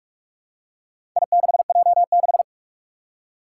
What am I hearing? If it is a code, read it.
Morse code, heard as a keyed tone, about 45 wpm, 700 Hz. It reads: I6J6
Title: Four Challenging Letter-Number Combinations 45wpm